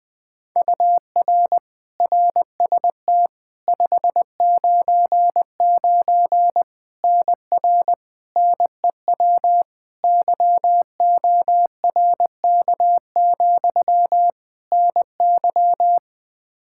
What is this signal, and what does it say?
Morse code, heard as a keyed tone, about 21 wpm, 700 Hz. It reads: UR RST 599 NR NEW YORK, NY
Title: K1USN SST QSO Practice 20wpm